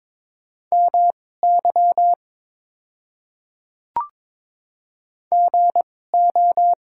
Morse code, heard as a keyed tone, about 22 wpm, spaced as for 16 wpm, 700 Hz. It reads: MY GO
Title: 2x Character Spacing - Top 100 Words - 22wpm